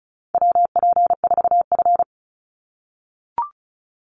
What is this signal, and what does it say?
Morse code, heard as a keyed tone, about 36 wpm, 700 Hz. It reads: WP4F
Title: Call Signs - 35wpm